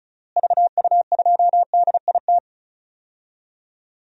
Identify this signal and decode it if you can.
Morse code, heard as a keyed tone, about 36 wpm, 700 Hz. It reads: VU2DIT
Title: DXCC-100 Call Signs - 6 Characters - Repeated 3x - 35wpm